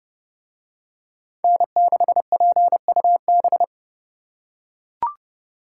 Morse code, heard as a keyed tone, about 27 wpm, 700 Hz. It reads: N6PUB